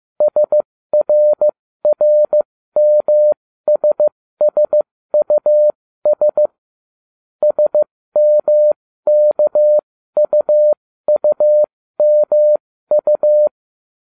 Morse code, heard as a keyed tone, about 15 wpm, spaced as for 13 wpm, 600 Hz. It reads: SRRMSSUS SMKUUMU